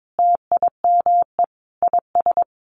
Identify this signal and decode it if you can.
Morse code, heard as a keyed tone, about 22 wpm, 700 Hz. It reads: TIME IS
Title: Sentences from Top 500 Words 22wpm